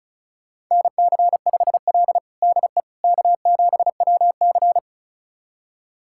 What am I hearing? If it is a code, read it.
Morse code, heard as a keyed tone, about 34 wpm, 700 Hz. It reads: NC5L DE K7WC
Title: USA Callsign DE Callsign - 4 Characters - 35wpm